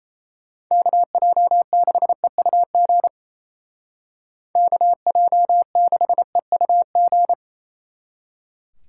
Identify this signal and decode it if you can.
Morse code, heard as a keyed tone, about 36 wpm, 700 Hz. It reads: KJ6EUG KJ6EUG